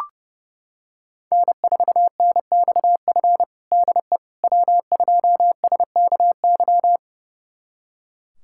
Morse code, heard as a keyed tone, about 32 wpm, 700 Hz. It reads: N4NXF DE W2SKY
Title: USA Callsign DE Callsign - 5 Characters - 30wpm